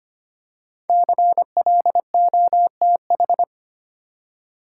Morse code, heard as a keyed tone, about 25 wpm, 700 Hz. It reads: CLOTH
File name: 5 Letter Words - First 3 Letters Given - 25wpm